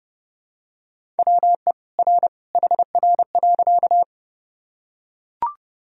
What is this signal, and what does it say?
Morse code, heard as a keyed tone, about 30 wpm, 700 Hz. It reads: WE R HR.
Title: Sentences from Top 200 Words and Abbreviations 30wpm